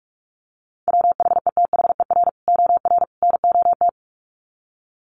Morse code, heard as a keyed tone, about 44 wpm, 700 Hz. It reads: WHETHER OR NOT
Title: Sets of 3 Words 45wpm